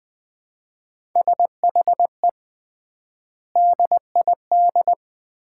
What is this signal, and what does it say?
Morse code, heard as a keyed tone, about 21 wpm, 700 Hz. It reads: SHE DID
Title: 3x Word-Spacing - Sets of 3 Words - 20wpm